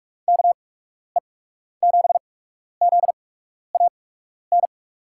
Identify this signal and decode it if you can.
Morse code, heard as a keyed tone, about 44 wpm, spaced as for 12 wpm, 700 Hz. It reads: KE7ZAN
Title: USA Call Signs - 6 Characters - 8x Character Spacing - 45wpm